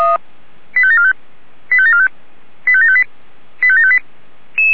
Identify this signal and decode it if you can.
Five-tone selective call (ZVEI), heard as quick runs of five tones, tones of about 80 ms. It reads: 85745 85745 85758 85758